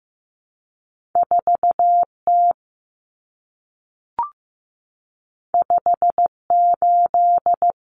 Morse code, heard as a keyed tone, about 15 wpm, 700 Hz. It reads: 4T 58
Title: Common Cut Numbers - 2-Digit 15wpm